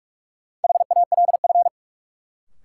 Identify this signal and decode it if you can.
Morse code, heard as a keyed tone, about 46 wpm, 700 Hz. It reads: HALF
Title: Sets of 3 Words - New Word Drill - 45wpm